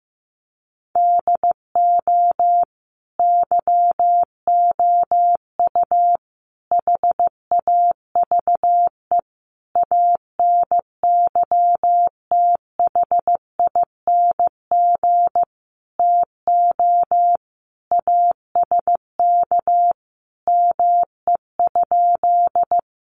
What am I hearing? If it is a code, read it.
Morse code, heard as a keyed tone, about 15 wpm, 700 Hz. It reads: DO YOU HAVE ANYTHING TO ASK ME?
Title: Sentences from Top 1000 Words 15wpm